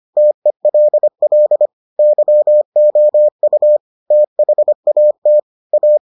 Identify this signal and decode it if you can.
Morse code, heard as a keyed tone, about 25 wpm, 600 Hz. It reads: TELL YOU THAT A